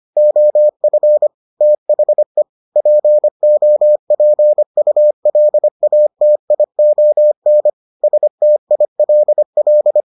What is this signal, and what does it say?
Morse code, heard as a keyed tone, about 25 wpm, 600 Hz. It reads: OF THE POPULATION STILL